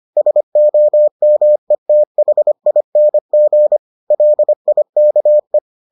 Morse code, heard as a keyed tone, about 25 wpm, 600 Hz. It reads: SOMETHING LIKE